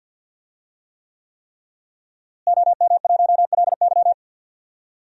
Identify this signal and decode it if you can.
Morse code, heard as a keyed tone, about 52 wpm, 700 Hz. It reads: OM1LY
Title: DXCC-100 Call Signs - 5 Characters - Repeated 3x - 50wpm